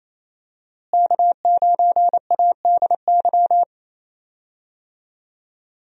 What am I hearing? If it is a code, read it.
Morse code, heard as a keyed tone, about 28 wpm, 700 Hz. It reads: K9ADY
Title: USA Call Signs - 5 Characters - Repeated 3x - 28wpm